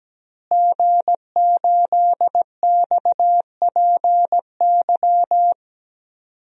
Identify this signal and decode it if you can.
Morse code, heard as a keyed tone, about 17 wpm, 700 Hz. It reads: G8XPY
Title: DXCC-100 Call Signs - 5 Characters - Repeated 3x - 17wpm